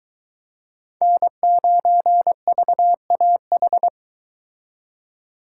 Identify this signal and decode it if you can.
Morse code, heard as a keyed tone, about 22 wpm, 700 Hz. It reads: N9VAH